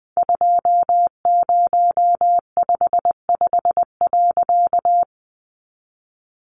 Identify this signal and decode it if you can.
Morse code, heard as a keyed tone, about 20 wpm, 700 Hz. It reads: 2055.
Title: A Sound Of Thunder 20wpm